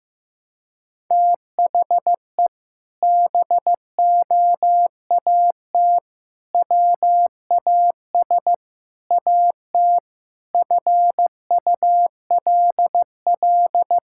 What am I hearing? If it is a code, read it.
Morse code, heard as a keyed tone, about 16 wpm, 700 Hz. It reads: THE BOAT WAS AT FULL
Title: Sentences from Top 1000 Words - Review Part 5 - 15wpm